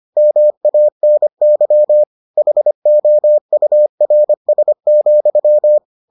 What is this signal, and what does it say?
Morse code, heard as a keyed tone, about 25 wpm, 600 Hz. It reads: MANY HOURS,